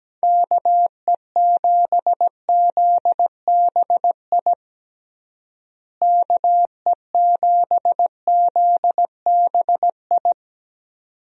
Morse code, heard as a keyed tone, about 17 wpm, 700 Hz. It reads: KE7ZBI KE7ZBI
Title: USA Call Signs - 6 Characters - Repeated 2x - 17wpm